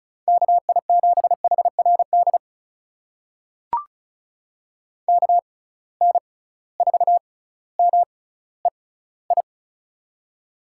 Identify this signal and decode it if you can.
Morse code, heard as a keyed tone, about 36 wpm, spaced as for 18 wpm, 700 Hz. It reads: KI7HRD KN4MEI
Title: USA Call Signs - 6 Characters - 6x Character Spacing - 35wpm